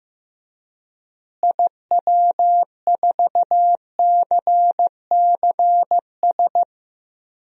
Morse code, heard as a keyed tone, about 16 wpm, 700 Hz. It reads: IW4CCS
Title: DXCC-100 Call Signs - 6 Characters - Repeated 3x - 15wpm